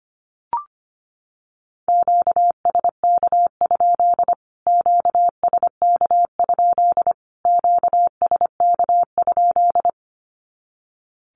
Morse code, heard as a keyed tone, about 25 wpm, 700 Hz. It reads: QSK? QSK? QSK?
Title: Q-Codes Repeated 3x 25wpm